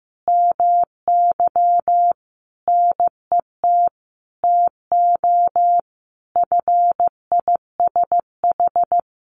Morse code, heard as a keyed tone, about 15 wpm, 700 Hz. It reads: MY NET TO FISH
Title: Sentences from Top 1400 Words 15wpm